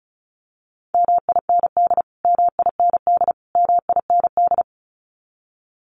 Morse code, heard as a keyed tone, about 34 wpm, 700 Hz. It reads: MIND MIND MIND
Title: Top 500 Words Repeated 35wpm